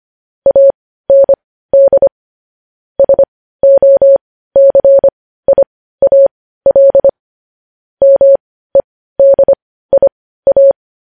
Morse code, heard as a keyed tone, about 25 wpm, spaced as for 15 wpm, 550 Hz. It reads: AND SOCIAL MEDIA
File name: podcast / News headlines-1x-25x15